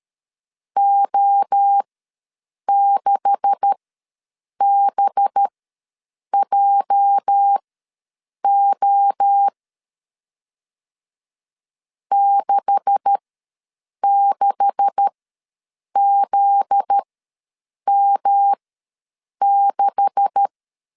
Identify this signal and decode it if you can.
Morse code, heard as a keyed tone, about 13 wpm, spaced as for 7 wpm, 800 Hz. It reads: O6BJO 66ZM6
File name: LEZIONI / 9 286 ws 12wpm 3351